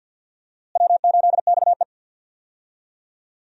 Morse code, heard as a keyed tone, about 46 wpm, 700 Hz. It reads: W8XE
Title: USA Call Signs - 4 Characters - Repeated 2x - 50wpm